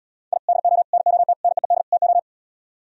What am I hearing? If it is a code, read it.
Morse code, heard as a keyed tone, about 78 wpm, 700 Hz. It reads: I DO NOT NEED TO